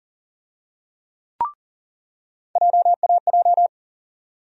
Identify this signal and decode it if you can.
Morse code, heard as a keyed tone, about 40 wpm, 700 Hz. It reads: JAJ
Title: USA Call Sign Postfixes - 3 Letters 40wpm